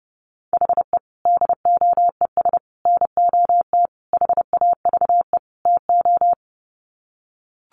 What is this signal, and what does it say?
Morse code, heard as a keyed tone, about 30 wpm, 700 Hz. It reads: HE DOES NOT HAVE TO
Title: Sets of 5 Words Encore 30wpm